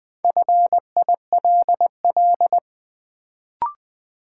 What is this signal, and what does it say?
Morse code, heard as a keyed tone, about 20 wpm, 700 Hz. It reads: FILL
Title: Sentences from Top 1500 Words - New Word Drill Alt - 20wpm